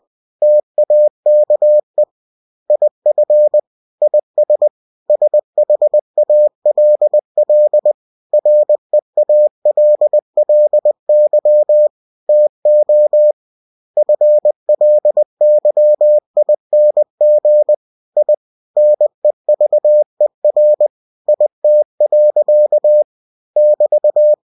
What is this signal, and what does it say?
Morse code, heard as a keyed tone, about 20 wpm, 600 Hz. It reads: TAKE IF IS SHALL REALLY TO FLYING I NEVER IT. <BT>
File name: cw-books / winnie-the-pooh / winnie-20wpm-page-0070